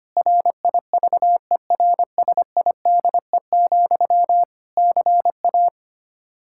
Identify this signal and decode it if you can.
Morse code, heard as a keyed tone, about 25 wpm, 700 Hz. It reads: RIVERSIDE, CA
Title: Practice QSO - Exchange 1 - Beginning - 25wpm